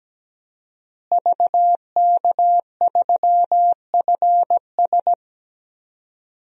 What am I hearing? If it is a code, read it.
Morse code, heard as a keyed tone, about 17 wpm, 700 Hz. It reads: VK3FS
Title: USA Call Signs - 5 Characters - Repeated 2x - 17wpm